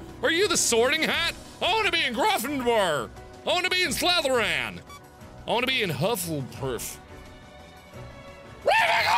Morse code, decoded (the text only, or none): none